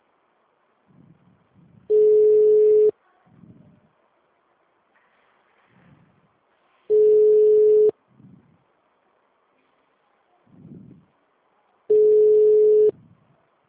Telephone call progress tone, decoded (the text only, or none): ringback tone